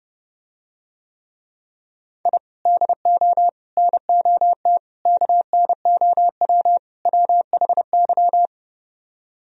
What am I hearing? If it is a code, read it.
Morse code, heard as a keyed tone, about 31 wpm, 700 Hz. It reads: I DO NOT KNOW WHY